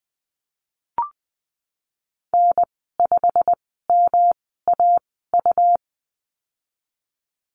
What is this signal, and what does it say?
Morse code, heard as a keyed tone, about 20 wpm, spaced as for 15 wpm, 700 Hz. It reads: N5MAU
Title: 147 Lesson 24g - Letter y - Call Signs 20wpm